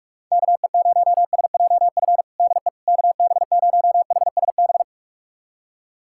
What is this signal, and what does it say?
Morse code, heard as a keyed tone, about 43 wpm, 700 Hz. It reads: KE0SJF DE KB0HSB
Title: USA Callsign DE Callsign - 6 Characters - 45wpm